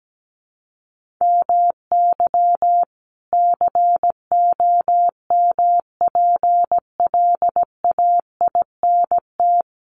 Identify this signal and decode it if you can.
Morse code, heard as a keyed tone, about 17 wpm, 700 Hz. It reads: MY COMPLAINT
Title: Sentences from Top 1400 Words 17wpm